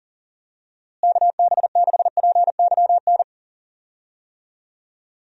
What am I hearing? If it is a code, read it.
Morse code, heard as a keyed tone, about 40 wpm, 700 Hz. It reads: KB6PYN
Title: USA Call Signs - 6 Characters - Repeated 2x - 40wpm